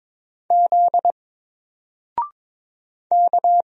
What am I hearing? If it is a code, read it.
Morse code, heard as a keyed tone, about 22 wpm, 700 Hz. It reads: Z K
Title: Single Character (Letter:Number:Punctuation:Prosign) Alt 22wpm